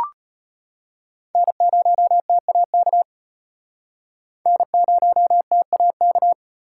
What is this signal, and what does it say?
Morse code, heard as a keyed tone, about 34 wpm, 700 Hz. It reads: N0TAK N0TAK